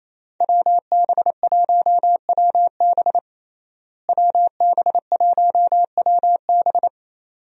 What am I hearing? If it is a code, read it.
Morse code, heard as a keyed tone, about 28 wpm, 700 Hz. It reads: WB1WB WB1WB
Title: USA Call Signs - 5 Characters - Repeated 2x - 28wpm